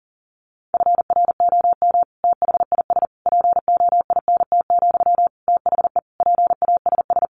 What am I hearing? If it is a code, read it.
Morse code, heard as a keyed tone, about 40 wpm, 700 Hz. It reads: FROM THIS POINT, THE PASS
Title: Sentences from Top 1200 Words 40wpm